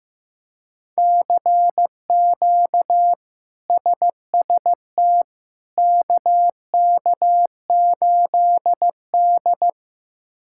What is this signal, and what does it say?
Morse code, heard as a keyed tone, about 15 wpm, 700 Hz. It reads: CQ SST KK8D
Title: K1USN SST QSO Practice - 15wpm